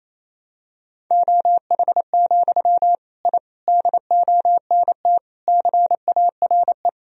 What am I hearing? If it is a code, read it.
Morse code, heard as a keyed tone, about 28 wpm, 700 Hz. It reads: OH, I DONT CARE
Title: Sentences from Top 1000 Words - Review Part 1 - 28wpm